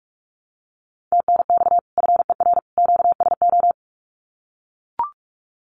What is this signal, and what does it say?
Morse code, heard as a keyed tone, about 44 wpm, 700 Hz. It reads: TNX FER QSO
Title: Common QSO Phrases 45wpm